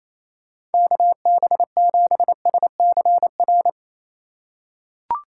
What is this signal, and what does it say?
Morse code, heard as a keyed tone, about 28 wpm, 700 Hz. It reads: KB7SCR
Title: USA Call Signs - 6 Characters - 4x Character Spacing - 28wpm